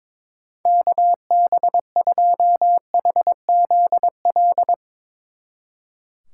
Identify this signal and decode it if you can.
Morse code, heard as a keyed tone, about 22 wpm, 700 Hz. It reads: KB2HZL
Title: Speed Racing - USA Call Signs 22wpm